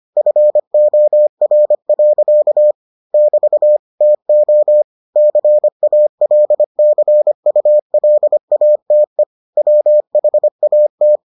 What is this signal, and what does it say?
Morse code, heard as a keyed tone, about 25 wpm, 600 Hz. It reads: FOR. <BT> TO CALCULATE WHAT